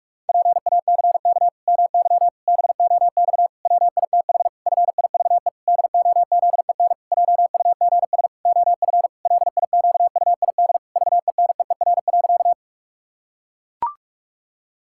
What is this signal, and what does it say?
Morse code, heard as a keyed tone, about 41 wpm, 700 Hz. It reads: PACK MY BOX WITH FIVE DOZEN JUGS OF LIQUID VENEER.
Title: Pangram Challenge v2 - 45wpm